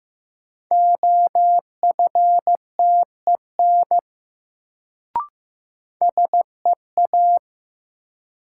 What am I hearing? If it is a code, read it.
Morse code, heard as a keyed tone, about 15 wpm, 700 Hz. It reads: OFTEN SEA
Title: Sentences from Top 300 Words - New Word Drill - 15wpm